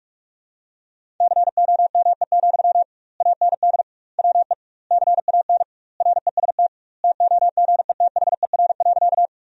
Morse code, heard as a keyed tone, about 41 wpm, 700 Hz. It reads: COME, AND WE CAN REST TOGETHER.
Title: 2x Word-Spacing - Sentences from Top 500 Words - 45wpm